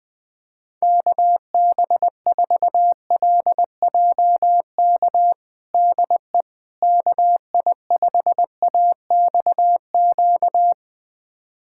Morse code, heard as a keyed tone, about 20 wpm, 700 Hz. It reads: KB4LJK DE KI5AXQ
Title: USA Callsign DE Callsign - 6 Characters - 20wpm